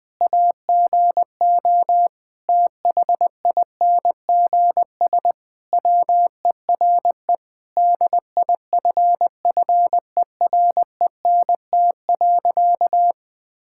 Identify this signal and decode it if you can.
Morse code, heard as a keyed tone, about 20 wpm, 700 Hz. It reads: AGO THINGS WERE DIFFERENT.
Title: Sentences with 4-Word Phrases - 20wpm